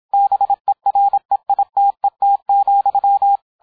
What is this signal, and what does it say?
Morse code, heard as a keyed tone, about 27 wpm, 800 Hz, with radio noise. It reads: BEREITET,